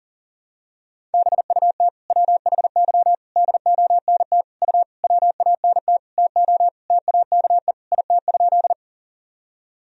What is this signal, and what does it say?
Morse code, heard as a keyed tone, about 37 wpm, 700 Hz. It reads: BUT WHY DONT U WANT TO TAKE IT?
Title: Sentences from Top 300 Words and Abbreviations 40wpm